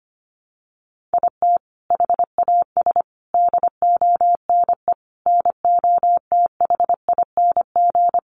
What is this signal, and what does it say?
Morse code, heard as a keyed tone, about 25 wpm, 700 Hz. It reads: IT HAS DONE NOTHING
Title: Sentences from Top 1100 Words 25wpm